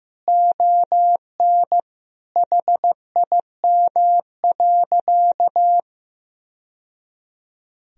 Morse code, heard as a keyed tone, about 16 wpm, 700 Hz. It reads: ON HIM.
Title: Sentences from Top 1000 Words - Review Part 1 - 15wpm